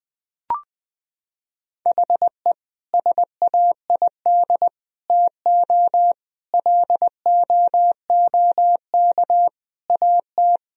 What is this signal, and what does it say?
Morse code, heard as a keyed tone, about 20 wpm, 700 Hz. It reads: HE SAID TO LOOK AT
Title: Sentences from Top 1000 Words - Review Part 1 - 20wpm